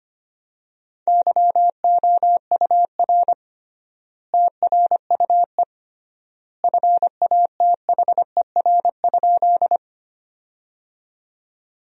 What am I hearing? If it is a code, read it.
Morse code, heard as a keyed tone, about 25 wpm, 700 Hz. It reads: YOUR TRUE FATHER?
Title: 3x Word-Spacing - Sentences from Top 400 Words - 25wpm